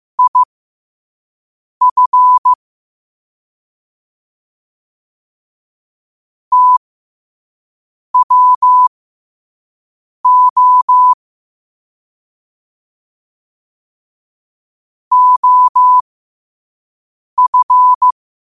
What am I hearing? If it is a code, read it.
Morse code, heard as a keyed tone, about 15 wpm, spaced as for 5 wpm, 1000 Hz. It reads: IF TWO OF
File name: qotdfiles / 5wpm